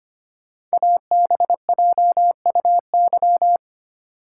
Morse code, heard as a keyed tone, about 25 wpm, 700 Hz. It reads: ABJUY
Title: Code Groups - 5 Letters - 25wpm